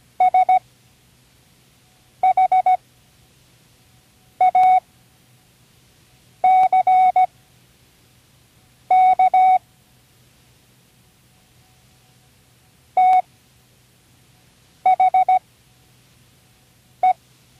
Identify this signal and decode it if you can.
Morse code, heard as a keyed tone, about 17 wpm, spaced as for 5 wpm, 740 Hz. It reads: SHACK THE
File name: CWResources / Learning Morse Code (from Tune in the World) / 29 words using all letters